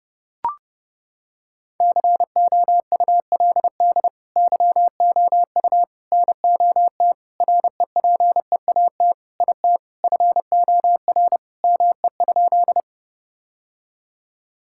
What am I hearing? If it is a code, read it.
Morse code, heard as a keyed tone, about 32 wpm, 700 Hz. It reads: COULD YOU NOT REPEAT IT FOR ME?
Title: Sentences from Top 1600 Words 30wpm